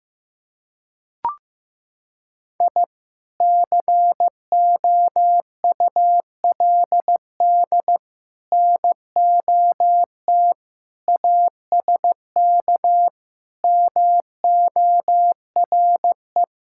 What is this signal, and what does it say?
Morse code, heard as a keyed tone, about 15 wpm, 700 Hz. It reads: I COULD NOT ASK MORE
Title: Sentences from Top 1000 Words - Review Part 5 - 15wpm